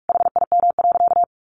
Morse code, heard as a keyed tone, about 44 wpm, 700 Hz. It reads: HIM.
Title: Sentences from Top 1300 Words 45wpm